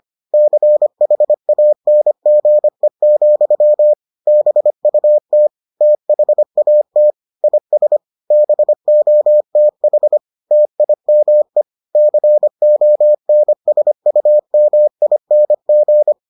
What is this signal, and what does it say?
Morse code, heard as a keyed tone, about 25 wpm, 600 Hz. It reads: CHANGE, BUT THAT IS BOTH TIME CONSUMING